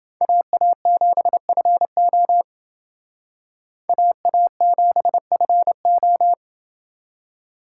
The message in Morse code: AA7FO AA7FO

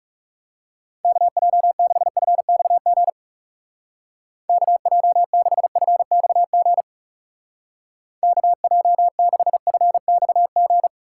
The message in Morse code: KJ6FXG KJ6FXG KJ6FXG

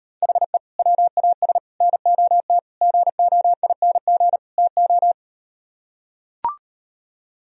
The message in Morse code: HE WAS NOT GOING TO

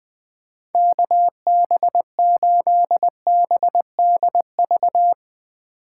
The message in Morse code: KB8BDV

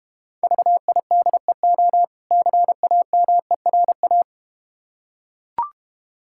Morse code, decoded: VIDEO CAMERA

